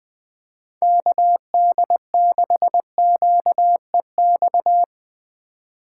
Morse code, heard as a keyed tone, about 20 wpm, 700 Hz. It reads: KD6QEX
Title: USA Call Signs - 6 Characters - 2x Character Spacing - 20wpm